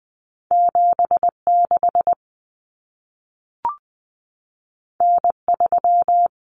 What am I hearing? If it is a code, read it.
Morse code, heard as a keyed tone, about 20 wpm, 700 Hz. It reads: 76 N3
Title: Common Cut Numbers - 2-Digit 20wpm